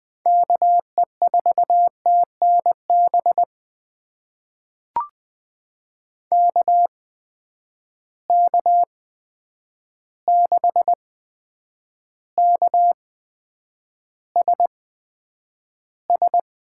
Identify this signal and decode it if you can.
Morse code, heard as a keyed tone, about 20 wpm, spaced as for 9 wpm, 700 Hz. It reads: KE4TNB KK6KSS
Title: USA Call Signs - 6 Characters - 8x Character Spacing - 20wpm